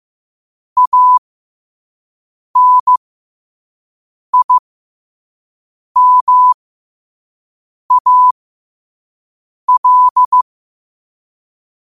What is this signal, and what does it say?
Morse code, heard as a keyed tone, about 15 wpm, spaced as for 5 wpm, 1000 Hz. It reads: ANIMAL
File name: qotdfiles / 5wpm